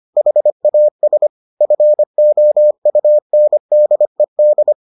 Morse code, heard as a keyed tone, about 25 wpm, 600 Hz. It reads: HAS FOUNDED